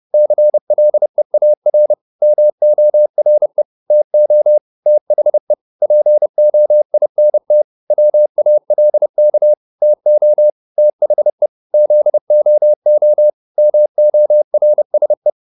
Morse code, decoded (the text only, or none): CLEAR MORE TO THE POINT WALK TO THE ZOO MORSE